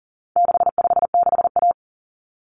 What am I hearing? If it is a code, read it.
Morse code, heard as a keyed tone, about 40 wpm, 700 Hz. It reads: 656A
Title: Common Cut Numbers - 4-Digit 40wpm